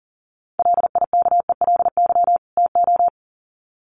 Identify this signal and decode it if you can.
Morse code, heard as a keyed tone, about 40 wpm, 700 Hz. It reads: LIKELY TO
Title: Sets of 2 Words 40wpm